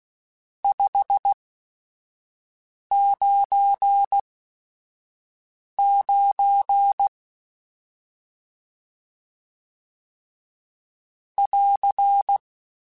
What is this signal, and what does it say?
Morse code, heard as a keyed tone, about 16 wpm, spaced as for 5 wpm, 800 Hz. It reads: 599 <AR>
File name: RSTs / RST-mp3s / Various RST @5